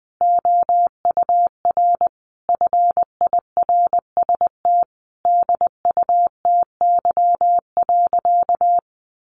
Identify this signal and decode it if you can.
Morse code, heard as a keyed tone, about 20 wpm, 700 Hz. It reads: OUR FIRST DUTY.
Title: Sentences from Top 1500 Words 20wpm